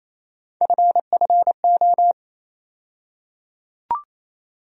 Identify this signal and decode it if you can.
Morse code, heard as a keyed tone, about 28 wpm, 700 Hz. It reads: FFO
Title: USA Call Sign Postfixes - 3 Letters 28wpm